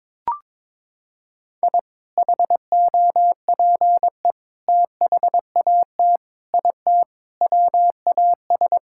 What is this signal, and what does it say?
Morse code, heard as a keyed tone, about 22 wpm, 700 Hz. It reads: I HOPE THAT IT WAS
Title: Sentences from Top 1000 Words - Review Part 5 - 22wpm